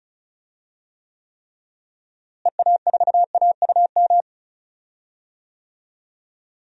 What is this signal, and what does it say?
Morse code, heard as a keyed tone, about 34 wpm, 700 Hz. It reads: EA4AUM
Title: DXCC-100 Call Signs - 6 Characters - Repeated 3x - 35wpm